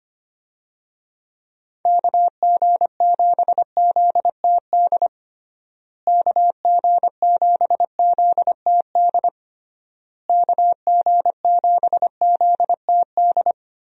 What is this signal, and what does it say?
Morse code, heard as a keyed tone, about 25 wpm, 700 Hz. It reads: KG7ZTD KG7ZTD KG7ZTD
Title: USA Call Signs - 6 Characters - Repeated 3x - 25wpm